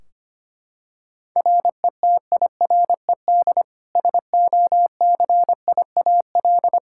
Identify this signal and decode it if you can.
Morse code, heard as a keyed tone, about 25 wpm, 700 Hz. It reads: RETIRED SOCIAL
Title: Practice QSO - Exchange 3 - More Info - 25wpm